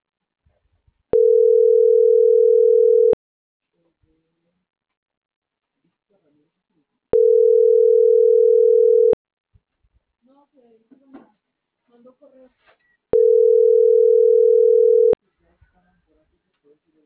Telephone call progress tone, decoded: ringback tone

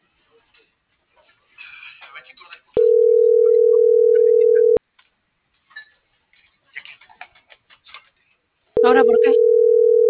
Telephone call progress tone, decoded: ringback tone

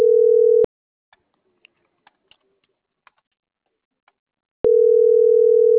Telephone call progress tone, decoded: ringback tone

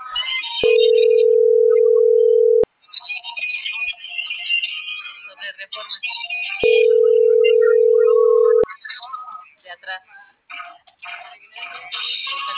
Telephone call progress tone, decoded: ringback tone

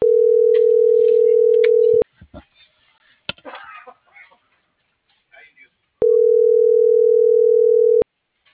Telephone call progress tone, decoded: ringback tone

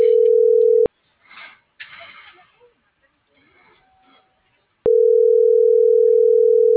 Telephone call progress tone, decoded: ringback tone

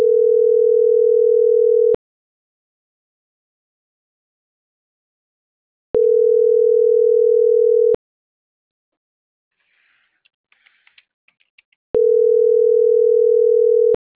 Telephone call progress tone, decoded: ringback tone

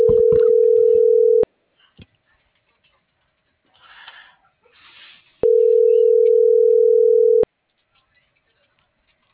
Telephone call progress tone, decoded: ringback tone

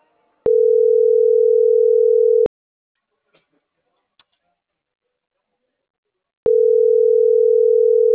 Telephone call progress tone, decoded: ringback tone

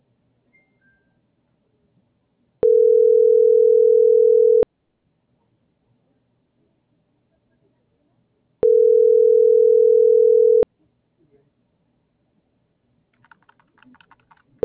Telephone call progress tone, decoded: ringback tone

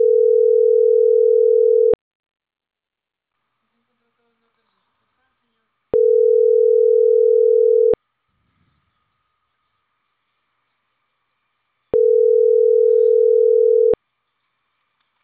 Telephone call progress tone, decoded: ringback tone

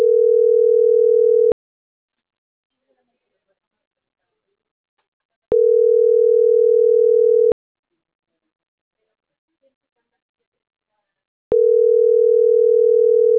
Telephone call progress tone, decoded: ringback tone

